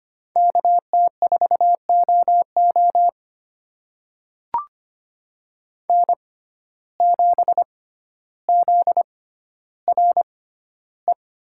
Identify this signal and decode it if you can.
Morse code, heard as a keyed tone, about 25 wpm, spaced as for 13 wpm, 700 Hz. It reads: KT4OO N7ZRE
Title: USA Call Signs - 5 Characters - 6x Character Spacing - 25wpm